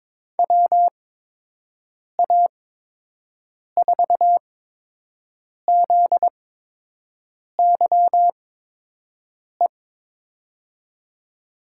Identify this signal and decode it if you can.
Morse code, heard as a keyed tone, about 22 wpm, spaced as for 6 wpm, 700 Hz. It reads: WA4ZYE
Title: USA Call Signs - 6 Characters - 8x Character Spacing - 22wpm